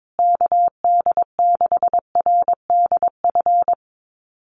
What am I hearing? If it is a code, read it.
Morse code, heard as a keyed tone, about 22 wpm, 700 Hz. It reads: KD6RDF
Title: USA Call Signs Repeated 3x 22wpm